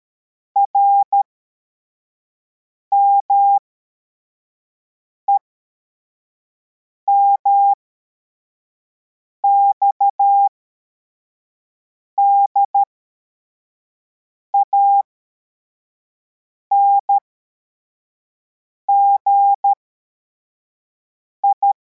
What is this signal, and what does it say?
Morse code, heard as a keyed tone, about 13 wpm, spaced as for 4 wpm, 800 Hz. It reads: RMEMX DANGI